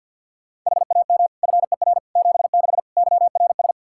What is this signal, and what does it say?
Morse code, heard as a keyed tone, about 49 wpm, 700 Hz. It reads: HAM FER 76 YRS
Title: Practice QSO - Exchange 3 - More Info - 50wpm